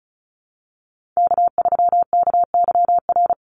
Morse code, heard as a keyed tone, about 34 wpm, 700 Hz. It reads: K3KYR
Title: NA SOTA Chaser Callsigns 35wpm